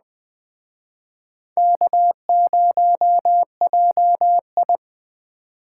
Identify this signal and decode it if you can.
Morse code, heard as a keyed tone, about 21 wpm, 700 Hz. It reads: K0JI
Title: Speed Racing - USA 1x2 Call Signs 17wpm